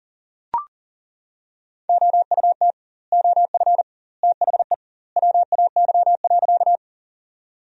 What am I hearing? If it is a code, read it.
Morse code, heard as a keyed tone, about 40 wpm, 700 Hz. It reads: OUT OF THE WAY.
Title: 2x Word-Spacing - Sentences from Top 100 Words - 40wpm